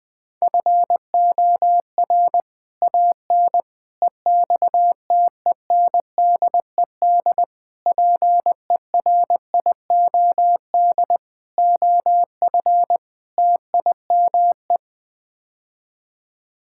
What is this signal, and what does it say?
Morse code, heard as a keyed tone, about 21 wpm, 700 Hz. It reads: FOR AN EXTENDED PERIOD OF TIME
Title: Sets of 6 Words - 20wpm